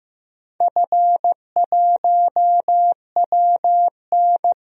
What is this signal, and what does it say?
Morse code, heard as a keyed tone, about 15 wpm, 700 Hz. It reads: F1WN
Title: DXCC-100 Call Signs - 4 Characters - 15wpm